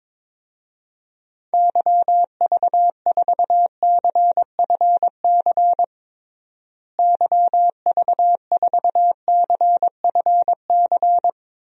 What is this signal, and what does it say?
Morse code, heard as a keyed tone, about 22 wpm, 700 Hz. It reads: YV4CFC YV4CFC
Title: DXCC-100 Call Signs - 6 Characters - Repeated 2x - 22wpm